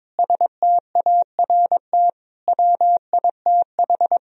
Morse code, heard as a keyed tone, about 22 wpm, 700 Hz. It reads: START WITH
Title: Sentences from Top 1000 Words - Review Part 1 - 22wpm